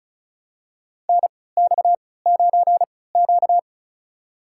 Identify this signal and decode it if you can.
Morse code, heard as a keyed tone, about 34 wpm, spaced as for 20 wpm, 700 Hz. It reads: NX9Q
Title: USA Call Signs - 4 Characters - 3x Character Spacing - 35wpm